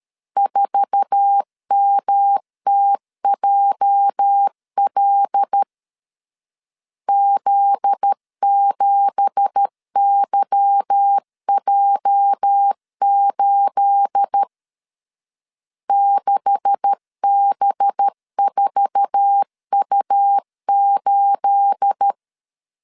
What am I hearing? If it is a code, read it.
Morse code, heard as a keyed tone, about 13 wpm, 800 Hz. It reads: 4MTJL Z7YJ8 6B4U8
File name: CW / DET / 1 9 1500C S A 12wpm 3800